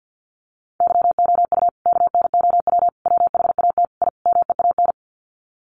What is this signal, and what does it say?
Morse code, heard as a keyed tone, about 50 wpm, 700 Hz. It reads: YOU KNOW WHAT I MEAN
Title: Sets of 5 Words 50wpm